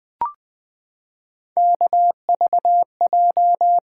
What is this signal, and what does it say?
Morse code, heard as a keyed tone, about 20 wpm, 700 Hz. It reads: KVJ
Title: Three Letter-Number Combinations - 20wpm